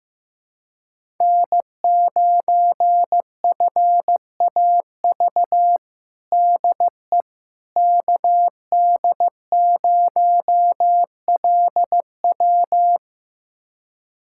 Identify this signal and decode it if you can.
Morse code, heard as a keyed tone, about 15 wpm, 700 Hz. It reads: N9FAV DE KD0LW
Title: USA Callsign DE Callsign - 5 Characters - 15wpm